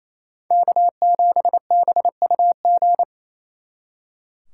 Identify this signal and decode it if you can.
Morse code, heard as a keyed tone, about 35 wpm, 700 Hz. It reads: K7BUG
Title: Speed Racing - USA 1x3 Call Signs 28wpm